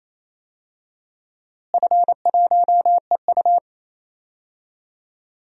Morse code, heard as a keyed tone, about 28 wpm, 700 Hz. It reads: F1EU
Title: DXCC-100 Call Signs - 4 Characters - Repeated 2x - 28wpm